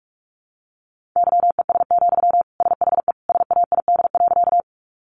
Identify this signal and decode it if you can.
Morse code, heard as a keyed tone, about 44 wpm, 700 Hz. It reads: YES, SHE SAID.
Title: Sentences from Top 500 Words 45wpm